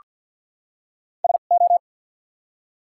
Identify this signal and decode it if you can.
Morse code, heard as a keyed tone, about 53 wpm, spaced as for 37 wpm, 700 Hz. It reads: SO
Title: 2x Character Spacing - Top 100 Words - 50wpm